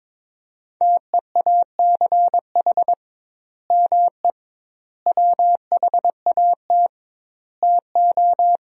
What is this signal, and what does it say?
Morse code, heard as a keyed tone, about 22 wpm, 700 Hz. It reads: TEACH ME WHAT TO
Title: 2x Word-Spacing - Sentences from Top 500 Words - 22wpm